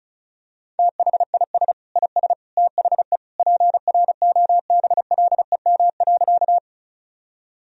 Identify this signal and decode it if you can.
Morse code, heard as a keyed tone, about 36 wpm, 700 Hz. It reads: THIS IS THE PROBLEM.
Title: Sentences from Top 1000 Words - Review Part 2 - 35wpm